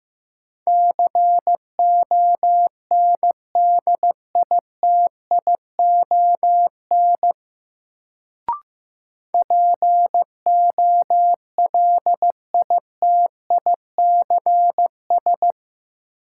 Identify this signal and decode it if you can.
Morse code, heard as a keyed tone, about 15 wpm, 700 Hz. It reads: CONDITION POLITICS
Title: Sentences from Top 1000 Words - New Word Drill - 15wpm